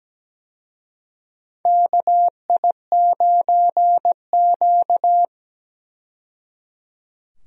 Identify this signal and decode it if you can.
Morse code, heard as a keyed tone, about 17 wpm, 700 Hz. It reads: KI9Q
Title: USA Call Signs - 4 Characters - 17wpm